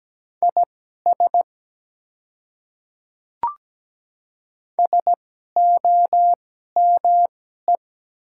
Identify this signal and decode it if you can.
Morse code, heard as a keyed tone, about 17 wpm, spaced as for 12 wpm, 700 Hz. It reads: IS SOME
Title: 2x Character Spacing - Top 100 Words - 17wpm